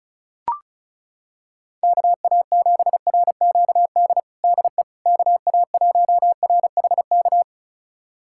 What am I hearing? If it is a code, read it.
Morse code, heard as a keyed tone, about 36 wpm, 700 Hz. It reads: KA7RQD DE KA1RHK
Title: USA Callsign DE Callsign - 6 Characters - 35wpm